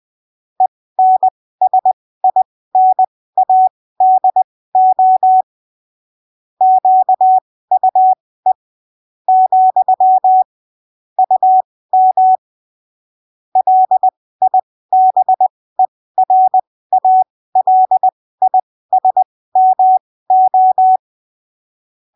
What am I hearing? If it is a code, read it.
Morse code, heard as a keyed tone, about 20 wpm, spaced as for 15 wpm, 750 Hz. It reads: ENSINADO QUE , UM LIBERALISMO
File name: arquivos / texto / texto 05 20ppm